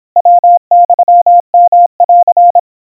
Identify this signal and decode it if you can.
Morse code (Japanese Wabun code), heard as a keyed tone, about 26 wpm, 700 Hz. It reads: ヤユヨン
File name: aiueo 90 cpm 700 Hz Crispy